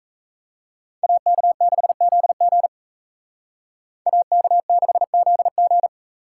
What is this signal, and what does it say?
Morse code, heard as a keyed tone, about 40 wpm, 700 Hz. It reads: AK6ZG AK6ZG